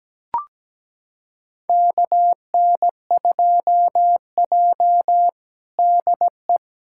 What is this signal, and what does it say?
Morse code, heard as a keyed tone, about 17 wpm, 700 Hz. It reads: KN2J DE
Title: USA Callsign DE Callsign - 4 Characters - 17wpm